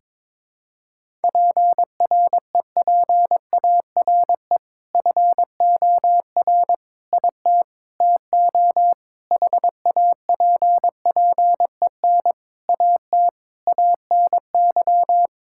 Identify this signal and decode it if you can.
Morse code, heard as a keyed tone, about 22 wpm, 700 Hz. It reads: PREPARE FOR IT TO HAPPEN AT ANY
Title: Sentences from Top 1000 Words - Review Part 6 - 22wpm